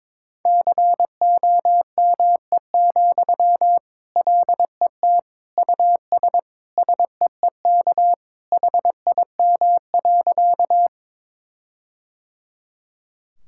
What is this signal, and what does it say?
Morse code, heard as a keyed tone, about 22 wpm, 700 Hz. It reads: COME, LET US SEEK HIM.
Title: Sentences from Top 1600 Words 22wpm